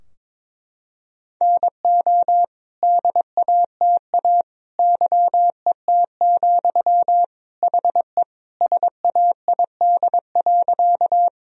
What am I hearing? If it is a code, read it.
Morse code, heard as a keyed tone, about 22 wpm, 700 Hz. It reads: NO DATA YET, HE SAID.
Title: Sentences from Top 1000 Words - Review Part 3 - 22wpm